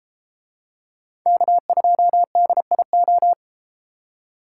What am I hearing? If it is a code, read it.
Morse code, heard as a keyed tone, about 32 wpm, 700 Hz. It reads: K2DIO